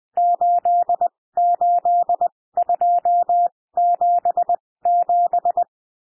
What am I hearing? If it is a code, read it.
Morse code, heard as a keyed tone, about 20 wpm, spaced as for 16 wpm, 700 Hz. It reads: 88277
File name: cwmp3 / N1 SOUND NUMBERS 20 wpm